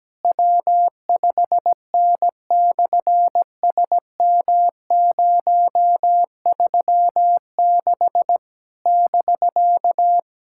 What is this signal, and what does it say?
Morse code, heard as a keyed tone, about 17 wpm, 700 Hz. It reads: W5N/SM036 <BK>
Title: SOTA QSO Practice 17wpm